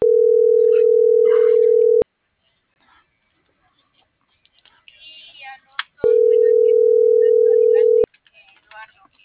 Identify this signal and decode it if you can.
Ringback tone (North America), 440 + 480 Hz.